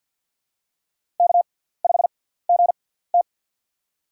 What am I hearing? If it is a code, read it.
Morse code, heard as a keyed tone, about 53 wpm, spaced as for 17 wpm, 690 Hz. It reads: K5GT
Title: USA Call Signs - 6x Character Spacing - 50wpm